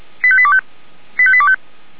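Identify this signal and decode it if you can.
Five-tone selective call (ZVEI), heard as quick runs of five tones, tones of about 70 ms. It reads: 85825 85825
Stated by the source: ZVEI five-tone selective call received on an SDR.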